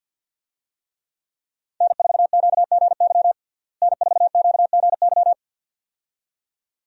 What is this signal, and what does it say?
Morse code, heard as a keyed tone, about 50 wpm, 700 Hz. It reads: N4QGY N4QGY
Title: USA Call Signs - 5 Characters - Repeated 2x - 50wpm